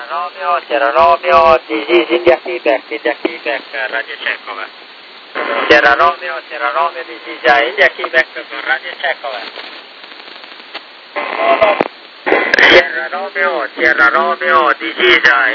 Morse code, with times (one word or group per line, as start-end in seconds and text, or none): none